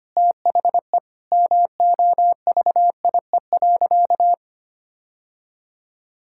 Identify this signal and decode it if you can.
Morse code, heard as a keyed tone, about 25 wpm, 700 Hz. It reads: THE MOVIE.
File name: practice / 25wpm / Sentences with 3-Word Phrases - 25wpm